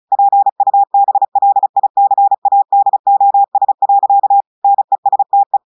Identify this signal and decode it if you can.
Morse code, heard as a keyed tone, about 36 wpm, 800 Hz. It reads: PUBLICADOS. NESTE
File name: audios / portugais / portugais texte en clair 1 vitesse 35 mots minutes